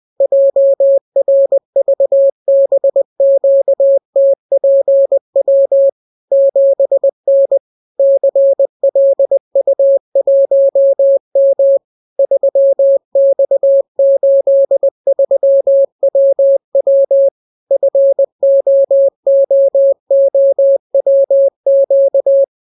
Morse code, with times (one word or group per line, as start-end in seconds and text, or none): 0.2-5.9 JRVBQTPW
6.3-7.6 7N
8.0-11.8 CLU1M
12.2-17.3 3X83WW
17.7-22.4 FOOOWQ